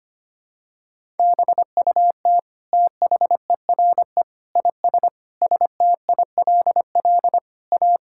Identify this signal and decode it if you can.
Morse code, heard as a keyed tone, about 25 wpm, 700 Hz. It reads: BUT THERE IS STILL A
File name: Sets of 9 Words - 25wpm